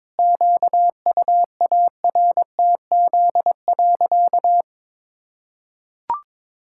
Morse code, heard as a keyed tone, about 22 wpm, 700 Hz. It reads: QUARTZ.
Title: Pangram Challenge v2 - 22wpm